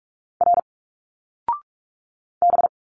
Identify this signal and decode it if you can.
Morse code, heard as a keyed tone, about 46 wpm, 700 Hz. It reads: R B
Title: Single Letters 45wpm